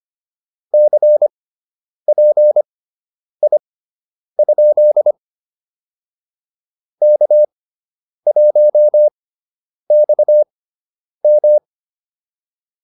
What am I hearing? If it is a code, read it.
Morse code, heard as a keyed tone, about 25 wpm, spaced as for 9 wpm, 600 Hz. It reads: CPI? K1XM